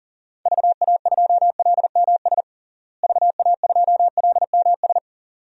Morse code, heard as a keyed tone, about 37 wpm, 700 Hz. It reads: VA2LMS VA2LMS
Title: DXCC-100 Call Signs - 6 Characters - Repeated 2x - 40wpm